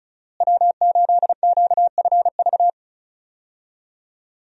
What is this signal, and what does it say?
Morse code, heard as a keyed tone, about 36 wpm, 700 Hz. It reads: W8QFV
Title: USA Call Signs - 5 Characters - Repeated 3x - 35wpm